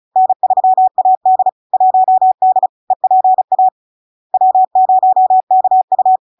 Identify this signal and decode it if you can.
Morse code, heard as a keyed tone, about 36 wpm, 750 Hz. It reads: N3AD 1D EPA W0KU
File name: LARK-FD-2022-Log-in-CW-35wpm-750-Hz